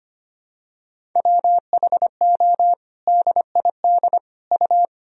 Morse code, heard as a keyed tone, about 25 wpm, 700 Hz. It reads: WHO DID U
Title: Sentences from Top 300 Words and Abbreviations 25wpm